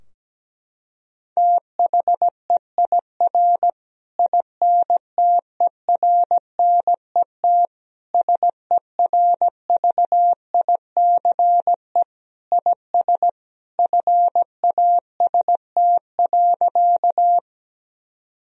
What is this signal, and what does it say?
Morse code, heard as a keyed tone, about 17 wpm, 700 Hz. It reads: THEIR INTERNET SERVICE IS FAST.
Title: Sentences from Top 1000 Words - Review Part 4 - 17wpm